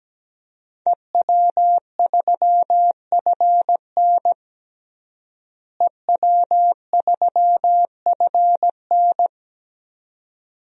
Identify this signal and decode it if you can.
Morse code, heard as a keyed tone, about 17 wpm, 700 Hz. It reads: EW3FN EW3FN
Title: DXCC-100 Call Signs - 5 Characters - Repeated 2x - 17wpm